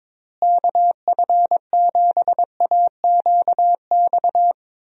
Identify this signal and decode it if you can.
Morse code, heard as a keyed tone, about 22 wpm, 700 Hz. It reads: KF7AQX
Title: USA Call Signs - 6 Characters - Repeated 3x - 22wpm